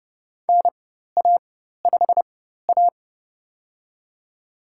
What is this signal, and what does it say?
Morse code, heard as a keyed tone, about 30 wpm, spaced as for 14 wpm, 700 Hz. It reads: NA5A
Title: USA Call Signs - 4x Character Spacing - 30wpm